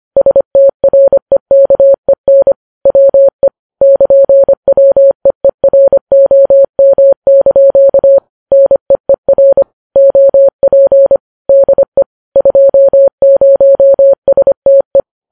Morse code, heard as a keyed tone, about 25 wpm, 550 Hz. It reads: STREKEN WE <KN>WEEROM) NEER OP DE 20STE
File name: Files Web tls / Morse on9fv / morse25kps